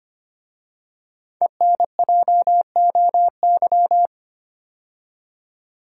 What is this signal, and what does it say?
Morse code, heard as a keyed tone, about 25 wpm, 700 Hz. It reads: ENJOY
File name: 5 Letter Words - First 3 Letters Given - 25wpm